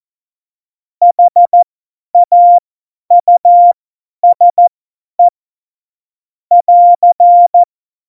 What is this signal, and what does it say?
Morse code, heard as a keyed tone, about 14 wpm, spaced as for 10 wpm, 700 Hz. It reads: HAUSE <AR>
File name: Morsekurs / Witze / Witz 27 Juni 17, Tempo 50 F70